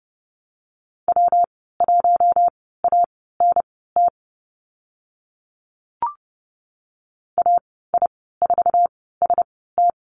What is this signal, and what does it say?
Morse code, heard as a keyed tone, about 30 wpm, spaced as for 17 wpm, 700 Hz. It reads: W1ANT AI4ST
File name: Lesson 19g - Letter w - Call Signs 30wpm